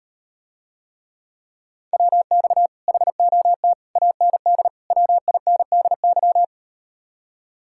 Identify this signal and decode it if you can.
Morse code, heard as a keyed tone, about 38 wpm, 700 Hz. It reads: WX HOT AND WINDY